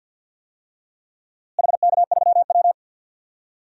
Morse code, heard as a keyed tone, about 53 wpm, 700 Hz. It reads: HK3W
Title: DXCC-100 Call Signs - 4 Characters - Repeated 3x - 50wpm